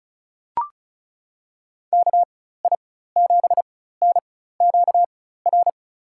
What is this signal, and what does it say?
Morse code, heard as a keyed tone, about 36 wpm, spaced as for 16 wpm, 700 Hz. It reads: KI7NQR
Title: USA Call Signs - 6 Characters - 4x Character Spacing - 35wpm